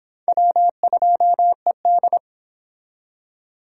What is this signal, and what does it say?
Morse code, heard as a keyed tone, about 25 wpm, 700 Hz. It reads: W2ED